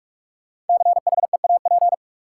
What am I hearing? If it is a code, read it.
Morse code, heard as a keyed tone, about 46 wpm, 700 Hz. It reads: CHEAP
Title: Sentences from Top 1200 Words - New Word Drill Alt - 45wpm